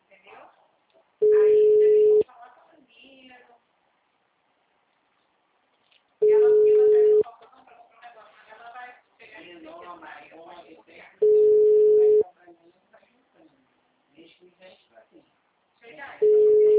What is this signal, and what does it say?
Ringback tone (Europe, 425 Hz), with speech.